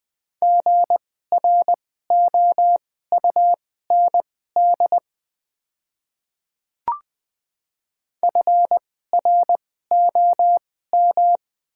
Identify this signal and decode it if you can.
Morse code, heard as a keyed tone, about 20 wpm, spaced as for 14 wpm, 700 Hz. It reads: GROUND FROM
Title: 2x Character Spacing - Top 500 Words - 20wpm